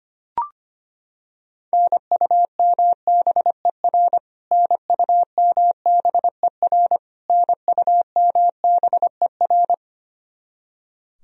Morse code, heard as a keyed tone, about 25 wpm, 700 Hz. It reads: NUMBER NUMBER NUMBER
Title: Top 100 Words Repeated 25wpm